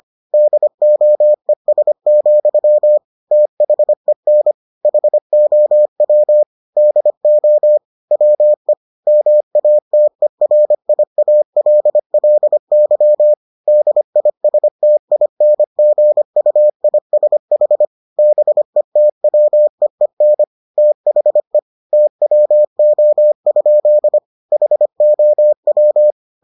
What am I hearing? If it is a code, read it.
Morse code, heard as a keyed tone, about 25 wpm, 600 Hz. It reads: DOES, THEN HOW DO WE MATERIALLY DISTINGUISH BETWEEN THE TWO? HOW